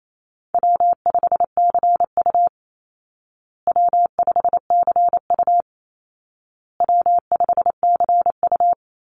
Morse code, heard as a keyed tone, about 28 wpm, 700 Hz. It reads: W5CU W5CU W5CU
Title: Call Signs Repeated 2x - 28wpm